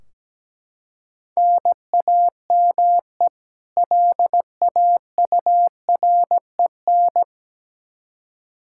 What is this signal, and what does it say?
Morse code, heard as a keyed tone, about 17 wpm, 700 Hz. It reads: NAME LAUREN
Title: K1USN SST QSO Practice 17wpm